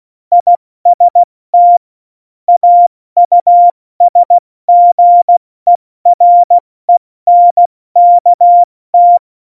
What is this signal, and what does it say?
Morse code, heard as a keyed tone, about 16 wpm, 700 Hz. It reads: IST AUSGERENKT
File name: Morsekurs/Witze/Witz 27. Oktober 2020, 70 F80